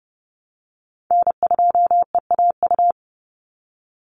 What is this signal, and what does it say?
Morse code, heard as a keyed tone, about 30 wpm, 700 Hz. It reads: N2EAU
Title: USA Call Signs 30wpm